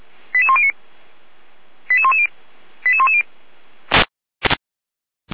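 Five-tone selective call (ZVEI), heard as quick runs of five tones, tones of about 70 ms. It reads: 88109 88110 88110
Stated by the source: ZVEI five-tone selective call 88012 received via SDR, fire brigade Feuerwehr Orlamünde.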